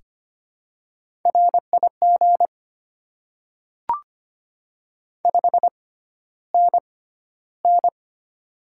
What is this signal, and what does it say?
Morse code, heard as a keyed tone, about 25 wpm, spaced as for 13 wpm, 700 Hz. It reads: RIG 5NN
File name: Top 20 QSO Elements - 6x Character Spacing - 25wpm